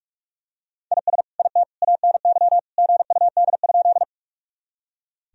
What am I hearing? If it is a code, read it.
Morse code, heard as a keyed tone, about 42 wpm, 700 Hz. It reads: IS IT ANY GUD?
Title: Sentences from Top 200 Words and Abbreviations 45wpm